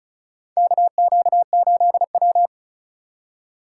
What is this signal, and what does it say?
Morse code, heard as a keyed tone, about 36 wpm, 700 Hz. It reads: KQ8W
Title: USA Call Signs - 4 Characters - 4x Character Spacing - 35wpm